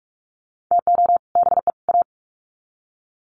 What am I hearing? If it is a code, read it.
Morse code, heard as a keyed tone, about 44 wpm, 700 Hz. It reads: TO BE A
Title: Sets of 3 Words 45wpm